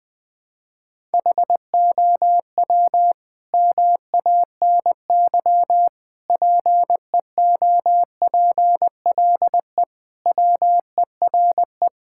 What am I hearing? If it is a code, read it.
Morse code, heard as a keyed tone, about 21 wpm, 700 Hz. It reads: HOW MANY PEOPLE WERE
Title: Sentences from Top 1000 Words - Review Part 5 - 20wpm